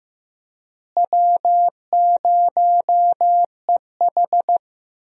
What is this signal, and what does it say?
Morse code, heard as a keyed tone, about 15 wpm, 700 Hz. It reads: W0EH
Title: USA Call Signs - 4 Characters - Repeated 3x - 15wpm